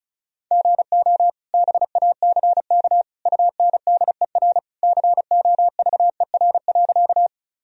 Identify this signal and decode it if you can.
Morse code, heard as a keyed tone, about 34 wpm, 700 Hz. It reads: GO BACK UNDER COVER.
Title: Sentences from Top 1000 Words - Review Part 1 - 35wpm